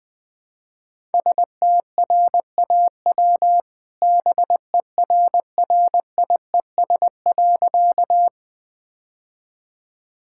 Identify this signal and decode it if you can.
Morse code, heard as a keyed tone, about 21 wpm, 700 Hz. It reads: STRAW BERRIES.
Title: All-Age Appropriate Jokes - 20wpm